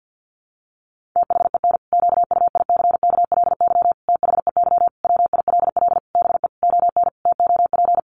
Morse code, heard as a keyed tone, about 48 wpm, 700 Hz. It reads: THEN QUICKLY THEY WILL BE ON TOP
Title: Sentences from Top 700 Words 50wpm